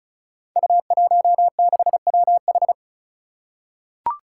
Morse code, heard as a keyed tone, about 36 wpm, 700 Hz. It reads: U16WH
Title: Five Challenging Letter-Number Combinations 35wpm